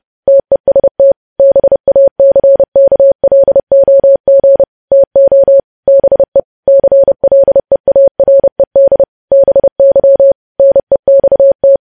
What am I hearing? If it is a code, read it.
Morse code, heard as a keyed tone, about 30 wpm, 550 Hz. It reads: TEST BACKLOG TO BE CLEARED BY NEXT
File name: podcast / News headlines-1x-30x30